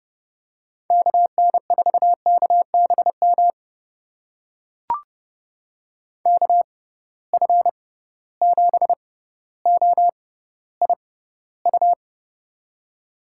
Morse code, heard as a keyed tone, about 30 wpm, spaced as for 15 wpm, 700 Hz. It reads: KN4KBM KF7OIU
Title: USA Call Signs - 6 Characters - 6x Character Spacing - 30wpm